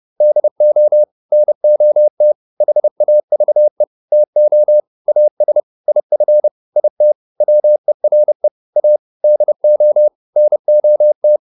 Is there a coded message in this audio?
Yes: Morse code from 0.2 to 11.4 s, heard as a keyed tone, about 30 wpm, 600 Hz. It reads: DO NOT HAVE TO AS IF IT WERE A DO NOT